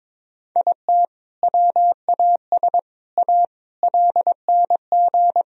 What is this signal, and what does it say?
Morse code, heard as a keyed tone, about 22 wpm, 700 Hz. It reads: IT WAS A LNG
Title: Sentences from Top 500 Words and Abbreviations 22wpm